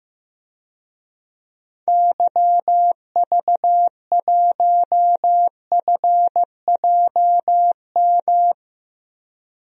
Morse code, heard as a keyed tone, about 15 wpm, 700 Hz. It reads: YV1FJM
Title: DXCC-100 Call Signs - 6 Characters - 15wpm